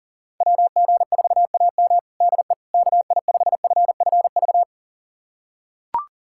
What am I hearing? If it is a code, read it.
Morse code, heard as a keyed tone, about 40 wpm, 700 Hz. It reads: WG4AM DE KI5FFV
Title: USA Callsign DE Callsign 40wpm